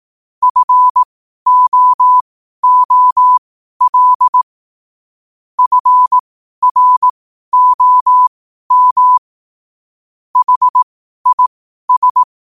Morse code, heard as a keyed tone, about 18 wpm, spaced as for 12 wpm, 1000 Hz. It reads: FOOL FROM HIS
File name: qotdfiles / 12wpm